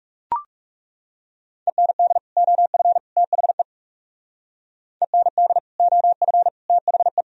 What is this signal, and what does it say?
Morse code, heard as a keyed tone, about 37 wpm, 700 Hz. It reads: END OF THE END OF THE
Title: Speed-Racing - Sets of 3 Words 30wpm